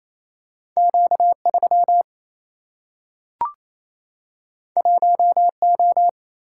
Morse code, heard as a keyed tone, about 28 wpm, 700 Hz. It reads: Q3 1O
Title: Single-Letter Single-Number 28wpm